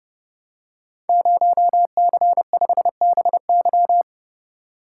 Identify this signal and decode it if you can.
Morse code, heard as a keyed tone, about 30 wpm, 700 Hz. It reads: 0C5BY
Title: Code Groups - 5 Letter:Numbers - 30wpm